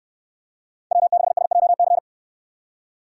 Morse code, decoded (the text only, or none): W6SJC